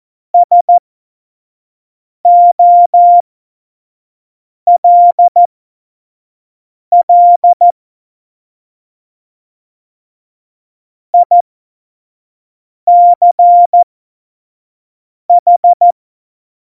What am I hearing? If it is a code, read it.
Morse code, heard as a keyed tone, about 14 wpm, spaced as for 5 wpm, 700 Hz. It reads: SOLL ICH
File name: Morsekurs / Witze / Witz 09. Januar 2018, 25 F70